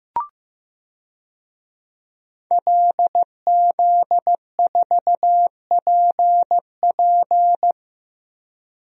Morse code, heard as a keyed tone, about 15 wpm, 700 Hz. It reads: LZ4PP
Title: DXCC-100 Call Signs - 5 Characters - Repeated 3x - 15wpm